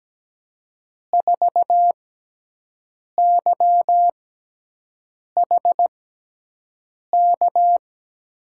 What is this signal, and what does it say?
Morse code, heard as a keyed tone, about 17 wpm, spaced as for 6 wpm, 700 Hz. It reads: 4YHK
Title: USA Call Signs - 6 Characters - 6x Character Spacing - 17wpm